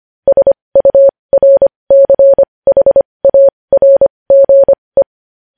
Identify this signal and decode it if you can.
Morse code, heard as a keyed tone, about 25 wpm, spaced as for 20 wpm, 550 Hz. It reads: SURCHARGE
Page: News Headlines in Morse Code at 20 WPM